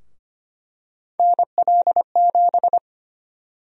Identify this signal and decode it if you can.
Morse code, heard as a keyed tone, about 25 wpm, 700 Hz. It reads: NL7
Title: USA Call Sign Prefixes - Repeated 2x - 25wpm